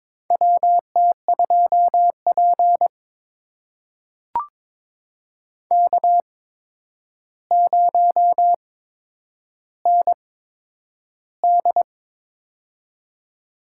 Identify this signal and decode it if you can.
Morse code, heard as a keyed tone, about 22 wpm, spaced as for 9 wpm, 700 Hz. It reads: WT2P K0ND
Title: USA Call Signs - 4 Characters - 8x Character Spacing - 22wpm